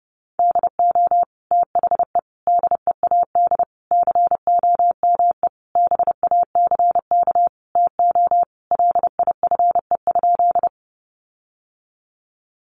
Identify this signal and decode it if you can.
Morse code, heard as a keyed tone, about 30 wpm, 700 Hz. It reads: DO THE DEAD COME BACK TO LIFE?
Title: Sentences from Top 1500 Words 30wpm